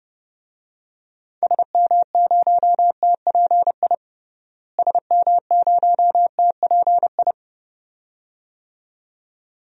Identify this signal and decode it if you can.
Morse code, heard as a keyed tone, about 32 wpm, 700 Hz. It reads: SM0TPI SM0TPI
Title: DXCC-100 Call Signs - 6 Characters - Repeated 2x - 30wpm